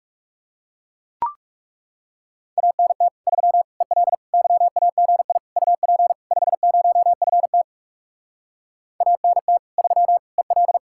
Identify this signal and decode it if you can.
Morse code, heard as a keyed tone, about 37 wpm, 700 Hz. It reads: ANT 3 EL YAGI UP 50FT ANT 3 EL
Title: Speed-Racing Common QSO Phrases - 30wpm